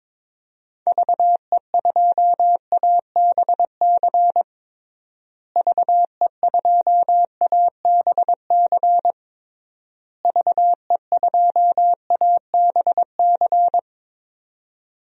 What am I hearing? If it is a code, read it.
Morse code, heard as a keyed tone, about 22 wpm, 700 Hz. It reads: VE2ABC VE2ABC VE2ABC
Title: DXCC-100 Call Signs - 6 Characters - Repeated 3x - 22wpm